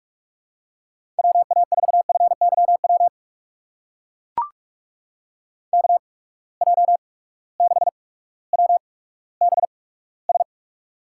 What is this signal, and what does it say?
Morse code, heard as a keyed tone, about 44 wpm, spaced as for 19 wpm, 700 Hz. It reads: WA4FYW KJ6WBS
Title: USA Call Signs - 6 Characters - 8x Character Spacing - 45wpm